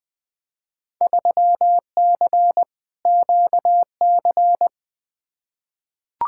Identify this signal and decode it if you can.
Morse code, heard as a keyed tone, about 21 wpm, 700 Hz. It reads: 3C QC
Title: Field Day Exchanges Encore - 20wpm